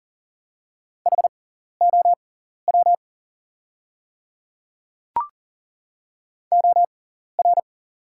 Morse code, heard as a keyed tone, about 40 wpm, spaced as for 14 wpm, 690 Hz. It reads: HOW OR
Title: 6x Character Spacing - Top 200 Words - 40wpm